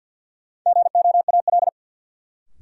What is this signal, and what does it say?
Morse code, heard as a keyed tone, about 50 wpm, 700 Hz. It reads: GOAL
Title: Sentences from Top 1000 Words - Review Part 5 - New Word Drill - 50wpm